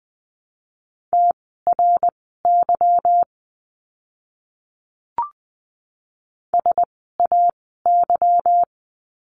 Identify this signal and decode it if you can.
Morse code, heard as a keyed tone, about 20 wpm, spaced as for 15 wpm, 700 Hz. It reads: TRY SAY
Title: 160 Lesson 26f - Letter g - Words 20wpm